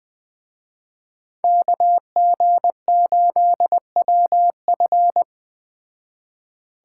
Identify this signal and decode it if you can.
Morse code, heard as a keyed tone, about 21 wpm, 700 Hz. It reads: KG8WF
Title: Speed Racing - USA 2x2 Call Signs 17wpm